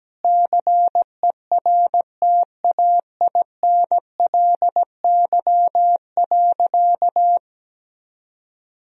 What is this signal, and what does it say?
Morse code, heard as a keyed tone, about 17 wpm, 700 Hz. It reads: CERTAINLY.
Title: Sentences from Top 1000 Words - Review Part 1 - 17wpm